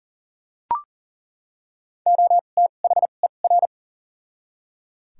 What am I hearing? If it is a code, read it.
Morse code, heard as a keyed tone, about 38 wpm, spaced as for 28 wpm, 700 Hz. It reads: OTHER
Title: 2x Character Spacing - Top 100 Words - 40wpm